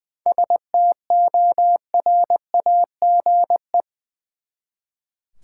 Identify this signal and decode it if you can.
Morse code, heard as a keyed tone, about 20 wpm, 700 Hz. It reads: STORAGE
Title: Seven Letter Words - 20wpm